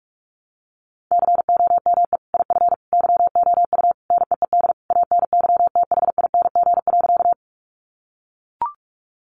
Morse code, heard as a keyed tone, about 44 wpm, 700 Hz. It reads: COME IF YOU NEED ANYTHING.
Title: Sentences from Top 1000 Words 45wpm